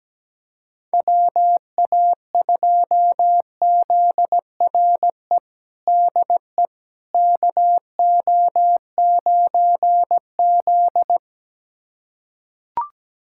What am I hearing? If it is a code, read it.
Morse code, heard as a keyed tone, about 17 wpm, 700 Hz. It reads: WA2ZRE DE KO9Z
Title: USA Callsign DE Callsign 17wpm